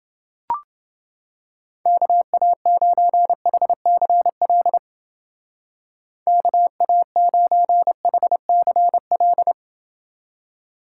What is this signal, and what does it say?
Morse code, heard as a keyed tone, about 29 wpm, 700 Hz. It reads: KA9HCL KA9HCL